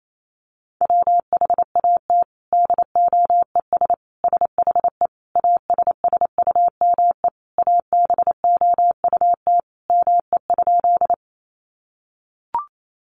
Morse code, heard as a keyed tone, about 28 wpm, 700 Hz. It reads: WHAT DOES SHE ASSUME ABOUT ME?
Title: Sentences from Top 1500 Words 28wpm